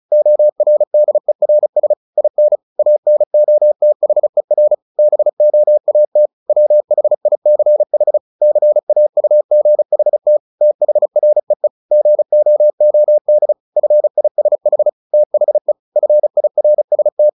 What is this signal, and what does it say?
Morse code, heard as a keyed tone, about 34 wpm, 600 Hz. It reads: ORDERS IN ANOTHER BOAT WHICH CAUGHT THREE GOOD FISH THE FIRST
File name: the old man and the sea / the old man and the sea 1 35wpm